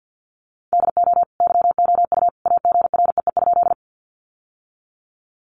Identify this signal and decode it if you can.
Morse code, heard as a keyed tone, about 50 wpm, 700 Hz. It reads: DO YOU AGREE?
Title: Sentences from Top 1100 Words 50wpm